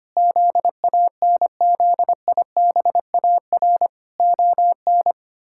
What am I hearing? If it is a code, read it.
Morse code, heard as a keyed tone, about 25 wpm, 700 Hz. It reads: ZANZIBAR ON
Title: Pangram Challenge v2 - 25wpm